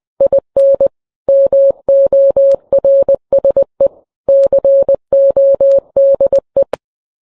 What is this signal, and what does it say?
Morse code, heard as a keyed tone, about 20 wpm, 550 Hz. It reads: IN MORSE CODE